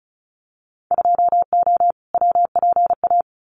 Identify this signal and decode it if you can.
Morse code, heard as a keyed tone, about 34 wpm, 700 Hz. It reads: 2O WPA
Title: Winter Field Day Exchanges 35wpm